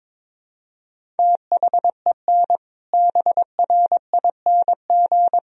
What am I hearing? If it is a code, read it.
Morse code, heard as a keyed tone, about 22 wpm, 700 Hz. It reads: THEN BRING
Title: Sentences from Top 500 Words and Abbreviations 22wpm